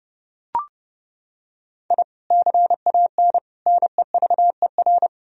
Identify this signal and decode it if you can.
Morse code, heard as a keyed tone, about 32 wpm, 700 Hz. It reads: I CAN NEVER
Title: Sentences from Top 1000 Words - Review Part 3 - 30wpm